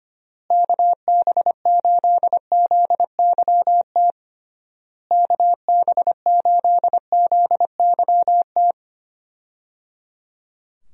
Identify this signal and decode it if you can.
Morse code, heard as a keyed tone, about 25 wpm, 700 Hz. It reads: KB8ZYT KB8ZYT
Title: USA Call Signs - 6 Characters - Repeated 3x - 25wpm